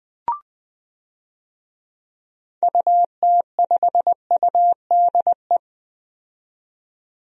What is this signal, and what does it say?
Morse code, heard as a keyed tone, about 20 wpm, 700 Hz. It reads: UT5UDE
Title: DXCC-100 Call Signs - 6 Characters - 20wpm